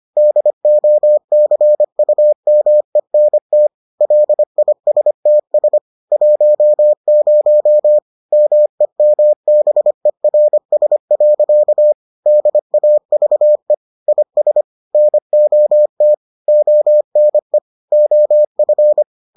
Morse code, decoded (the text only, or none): DOCUMENT LISTS 10 MEMBERS. DAVE IS NOT ONE OF